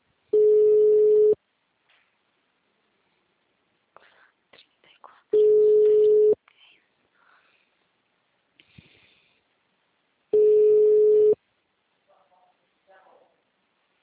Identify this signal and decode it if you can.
Ringback tone (Europe, 425 Hz), with speech.